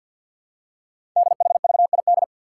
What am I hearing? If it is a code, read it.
Morse code, heard as a keyed tone, about 48 wpm, 700 Hz. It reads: DHVID
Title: Code Groups - 5 Letters - 50wpm